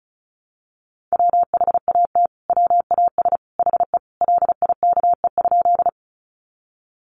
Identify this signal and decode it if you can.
Morse code, heard as a keyed tone, about 34 wpm, 700 Hz. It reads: WHAT WAS HE LIKE?
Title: Sentences from Top 100 Words 35wpm